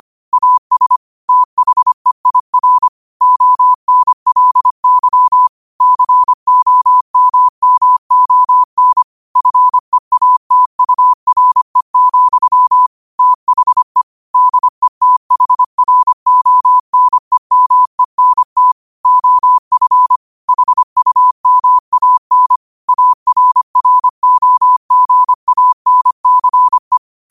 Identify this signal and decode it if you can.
Morse code, heard as a keyed tone, about 25 wpm, 1000 Hz. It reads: AS THEIR ONLY COMMON FEATURE, THE DETHRONEMENT OF HUMAN ARROGANCE
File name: qotdfiles / 25wpm